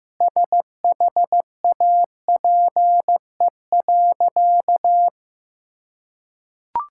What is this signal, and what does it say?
Morse code, heard as a keyed tone, about 15 wpm, 700 Hz. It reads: SHAPE.
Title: Sentences from Top 1000 Words - Review Part 6 - 15wpm